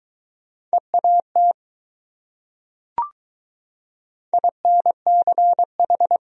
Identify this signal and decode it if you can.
Morse code, heard as a keyed tone, about 22 wpm, 700 Hz. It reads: EAT INCH